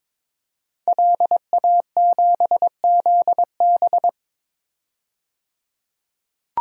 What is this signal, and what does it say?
Morse code, heard as a keyed tone, about 22 wpm, 700 Hz. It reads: LA7ZB
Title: DXCC-100 Call Signs - 5 Characters - 22wpm